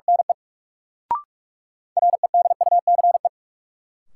Morse code, heard as a keyed tone, about 46 wpm, 700 Hz. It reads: NE REDUCE
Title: Sentences from Top 1100 Words - New Word Drill Alt - 45wpm